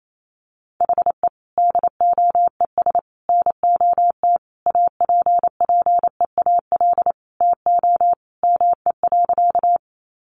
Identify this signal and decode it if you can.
Morse code, heard as a keyed tone, about 28 wpm, 700 Hz. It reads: HE DOES NOT APPEAL TO ME.
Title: Sentences from Top 1300 Words 28wpm